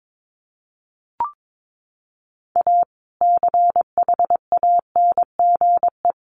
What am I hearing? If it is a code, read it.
Morse code, heard as a keyed tone, about 22 wpm, 700 Hz. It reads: A CHANGE
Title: Sentences from Top 1400 Words 22wpm